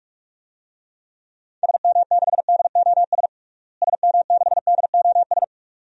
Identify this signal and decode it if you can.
Morse code, heard as a keyed tone, about 46 wpm, 700 Hz. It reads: SM6DOS SM6DOS
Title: DXCC-100 Call Signs - 6 Characters - Repeated 2x - 45wpm